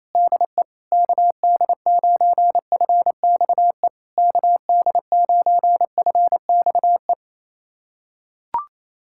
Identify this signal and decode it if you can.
Morse code, heard as a keyed tone, about 28 wpm, 700 Hz. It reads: DE KD9FXE KD9FXE
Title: SOTA QSO Practice 28wpm